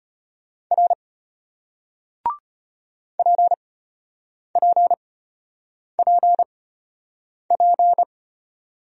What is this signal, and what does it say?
Morse code, heard as a keyed tone, about 34 wpm, spaced as for 8 wpm, 700 Hz. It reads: R PPPP